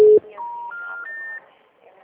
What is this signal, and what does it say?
Special information tone, with noise.